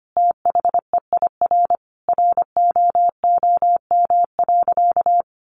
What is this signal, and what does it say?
Morse code, heard as a keyed tone, about 25 wpm, 700 Hz. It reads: THEIR ROOM.
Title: Easy Phrases 25wpm